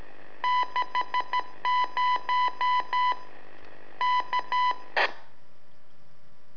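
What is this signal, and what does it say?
Morse code, heard as a keyed tone, about 13 wpm, 1000 Hz. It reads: 60 K
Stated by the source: Telemetry in Morse from LU7AA and LU4EZ, AMSAT and Turma Mar del Plata, sent from a balloon.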